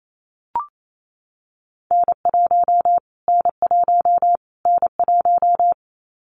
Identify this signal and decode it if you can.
Morse code, heard as a keyed tone, about 28 wpm, 700 Hz. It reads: N1 N1 N1
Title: Callsign Prefixes for Top 25 DXCC - Repeated 3x - 28wpm